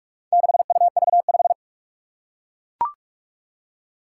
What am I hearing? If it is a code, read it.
Morse code, heard as a keyed tone, about 41 wpm, 700 Hz. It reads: 6UV5
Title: Four Challenging Letter-Number Combinations 45wpm